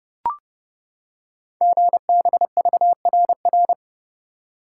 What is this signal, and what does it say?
Morse code, heard as a keyed tone, about 30 wpm, 700 Hz. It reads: GBVRR
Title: Code Groups - 5 Letters - 30wpm